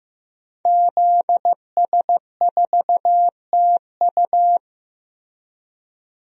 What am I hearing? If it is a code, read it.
Morse code, heard as a keyed tone, about 15 wpm, 700 Hz. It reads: ZS4TU
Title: DXCC-100 Call Signs - 5 Characters - Repeated 3x - 15wpm